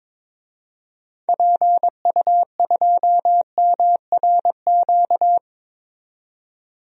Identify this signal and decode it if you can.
Morse code, heard as a keyed tone, about 22 wpm, 700 Hz. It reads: PU2MRQ
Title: DXCC-100 Call Signs - 6 Characters - 22wpm